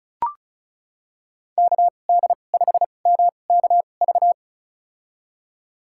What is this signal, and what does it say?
Morse code, heard as a keyed tone, about 36 wpm, spaced as for 25 wpm, 700 Hz. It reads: KD5MKV
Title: USA Call Signs - 2x Character Spacing - 35wpm